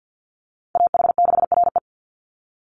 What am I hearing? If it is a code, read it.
Morse code, heard as a keyed tone, about 48 wpm, 700 Hz. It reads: AH6RE
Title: Call Signs Repeated 3x - 50wpm